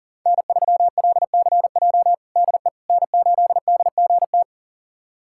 Morse code, heard as a keyed tone, about 40 wpm, 700 Hz. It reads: N3LCJ DE N8DGT
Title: USA Callsign DE Callsign 40wpm